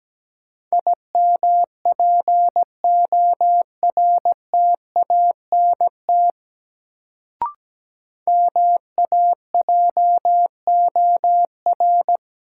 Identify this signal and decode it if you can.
Morse code, heard as a keyed tone, about 17 wpm, 700 Hz. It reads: IMPORTANT MAJOR
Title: Sentences from Top 600 Words - New Word Drill - 17wpm